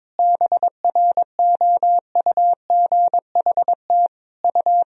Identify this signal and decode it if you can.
Morse code, heard as a keyed tone, about 22 wpm, 700 Hz. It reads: BROUGHT U
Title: Sentences from Top 500 Words and Abbreviations 22wpm